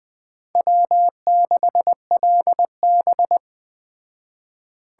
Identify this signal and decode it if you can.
Morse code, heard as a keyed tone, about 20 wpm, 700 Hz. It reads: W6LB
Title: Speed Racing - USA 1x2 Call Signs 17wpm